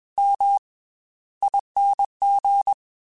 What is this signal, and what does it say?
Morse code, heard as a keyed tone, about 21 wpm, 800 Hz. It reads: M ING
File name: Special INGs / Special INGs-mp3s / Slow ing @20